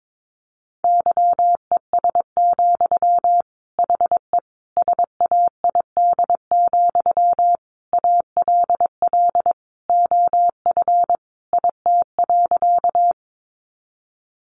Morse code, heard as a keyed tone, about 22 wpm, 700 Hz. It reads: YES, HE SAID, ALL OF IT.
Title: Sentences from Top 500 Words 22wpm